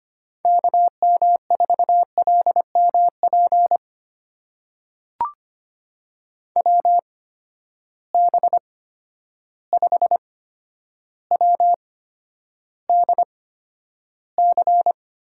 Morse code, heard as a keyed tone, about 25 wpm, spaced as for 11 wpm, 700 Hz. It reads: KM4LMP WB5WDC
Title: USA Call Signs - 8x Character Spacing - 25wpm